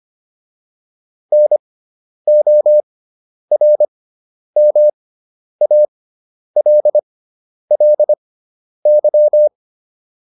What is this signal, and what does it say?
Morse code, heard as a keyed tone, about 25 wpm, spaced as for 10 wpm, 600 Hz. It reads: NORMALLY